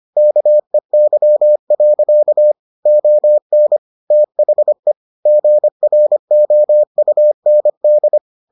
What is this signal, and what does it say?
Morse code, heard as a keyed tone, about 25 wpm, 600 Hz. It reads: KEY. ON THE GROUND